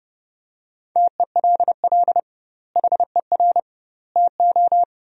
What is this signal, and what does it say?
Morse code, heard as a keyed tone, about 30 wpm, 700 Hz. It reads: TELL HER TO
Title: 2x Word-Spacing - Sentences from Top 300 Words - 30wpm